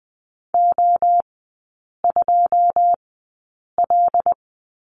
Morse code, heard as a keyed tone, about 20 wpm, spaced as for 8 wpm, 690 Hz. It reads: O2L
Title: 081 Lesson 14e - Number 2 - Three Characters 20wpm